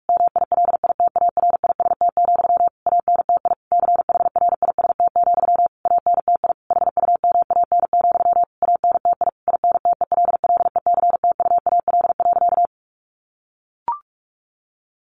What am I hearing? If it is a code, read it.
Morse code, heard as a keyed tone, about 44 wpm, 700 Hz. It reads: MILITARIST, ANTI CHRIST, ANTI HUMAN, ANTI INTELLECTUAL.
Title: A Sound Of Thunder 45wpm